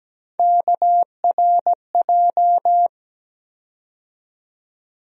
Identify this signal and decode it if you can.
Morse code, heard as a keyed tone, about 17 wpm, 700 Hz. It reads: KRJ
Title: USA Call Sign Postfixes - 3 Letters 17wpm